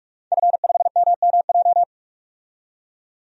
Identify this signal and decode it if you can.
Morse code, heard as a keyed tone, about 48 wpm, 700 Hz. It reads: F5MMJ
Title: DXCC-100 Call Signs - 5 Characters - 45wpm